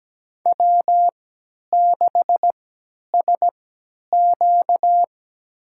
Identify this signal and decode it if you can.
Morse code, heard as a keyed tone, about 17 wpm, spaced as for 10 wpm, 700 Hz. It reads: W6SQ
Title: USA Call Signs - 4 Characters - 3x Character Spacing - 17wpm